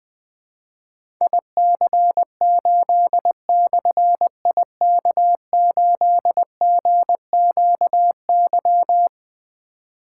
Morse code, heard as a keyed tone, about 20 wpm, 700 Hz. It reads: IC8/IK8GQY
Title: Difficult Call Signs 20wpm